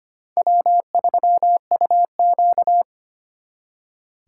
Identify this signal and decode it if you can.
Morse code, heard as a keyed tone, about 32 wpm, 700 Hz. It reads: W3UQ